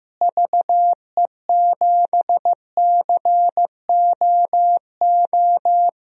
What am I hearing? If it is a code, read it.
Morse code, heard as a keyed tone, about 15 wpm, 700 Hz. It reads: VE7COO
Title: DXCC-100 Call Signs - 6 Characters - Repeated 3x - 15wpm